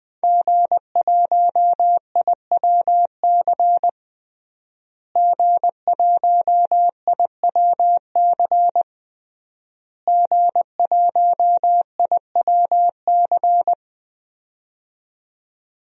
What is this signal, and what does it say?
Morse code, heard as a keyed tone, about 20 wpm, 700 Hz. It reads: G1IWC G1IWC G1IWC
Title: DXCC-100 Call Signs - 5 Characters - Repeated 3x - 20wpm